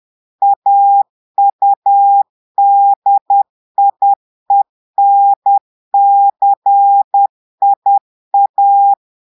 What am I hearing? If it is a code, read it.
Morse code, heard as a keyed tone, about 10 wpm, 800 Hz. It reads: AUDIENCIA
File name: portugais texte en clair 2 vitesse 10 mots minutes